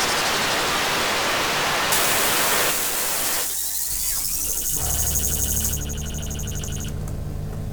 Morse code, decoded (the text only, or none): none